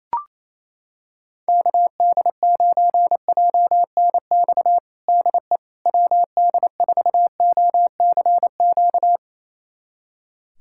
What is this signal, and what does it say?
Morse code, heard as a keyed tone, about 28 wpm, 700 Hz. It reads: KD9JNX DE WD4OCQ
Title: USA Callsign DE Callsign 28wpm